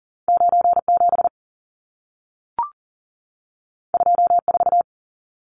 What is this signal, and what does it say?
Morse code, heard as a keyed tone, about 40 wpm, 700 Hz. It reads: 97 24
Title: Numbers - 2-Digit 40wpm